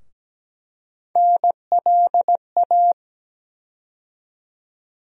Morse code, heard as a keyed tone, about 17 wpm, 700 Hz. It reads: NLA
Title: USA Call Sign Postfixes - 3 Letters 17wpm